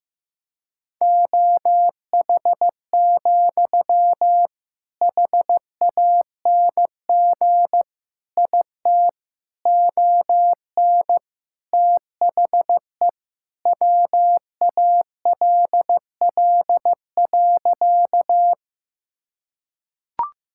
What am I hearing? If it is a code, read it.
Morse code, heard as a keyed tone, about 15 wpm, 700 Hz. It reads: OH, HANG IT ON THE WALL.
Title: Sentences from Top 1600 Words 15wpm